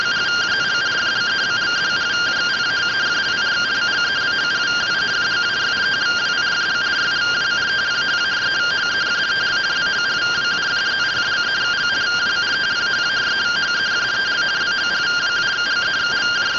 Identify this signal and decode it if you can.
Two-tone teleprinter signal, about 170 Hz shift.